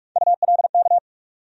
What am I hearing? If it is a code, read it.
Morse code, heard as a keyed tone, about 46 wpm, 700 Hz. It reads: ULK
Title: USA Call Sign Postfixes - 3 Letters 45wpm